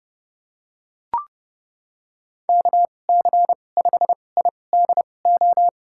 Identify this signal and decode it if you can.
Morse code, heard as a keyed tone, about 32 wpm, spaced as for 22 wpm, 700 Hz. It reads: KC5IDO
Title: USA Call Signs - 2x Character Spacing - 30wpm